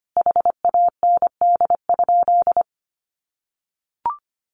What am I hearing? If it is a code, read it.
Morse code, heard as a keyed tone, about 25 wpm, 700 Hz. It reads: HAND?
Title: Sentences from Top 200 Words 25wpm